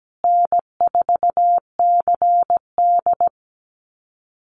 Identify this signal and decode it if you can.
Morse code, heard as a keyed tone, about 17 wpm, 700 Hz. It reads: N4CD
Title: Easy Call Signs Repeated 3x - 17wpm